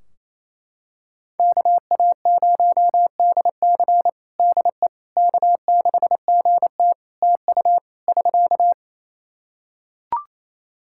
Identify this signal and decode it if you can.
Morse code, heard as a keyed tone, about 28 wpm, 700 Hz. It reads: KA0DC DE K6GT TU <SK>
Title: Practice QSO - Exchange 4 - Ending - 28wpm